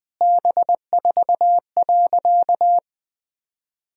Morse code, heard as a keyed tone, about 21 wpm, 700 Hz. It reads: B4.
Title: Sentences from Top 400 Words and Abbreviations 20wpm